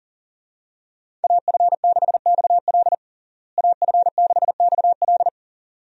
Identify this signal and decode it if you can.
Morse code, heard as a keyed tone, about 40 wpm, 700 Hz. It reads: AF6XL AF6XL
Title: USA Call Signs - 5 Characters - Repeated 3x - 40wpm